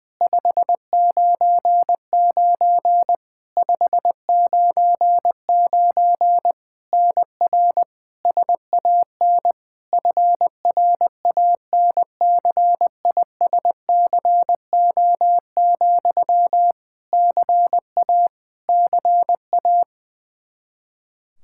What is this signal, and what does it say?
Morse code, heard as a keyed tone, about 21 wpm, 700 Hz. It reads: 599 599 NR SAN FRANCISCO, CA CA
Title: Practice QSO - Exchange 1 - Beginning - 20wpm